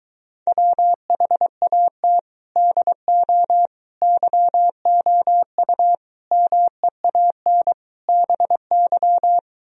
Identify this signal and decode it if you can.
Morse code, heard as a keyed tone, about 23 wpm, 700 Hz. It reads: WHAT DO YOU MEAN BY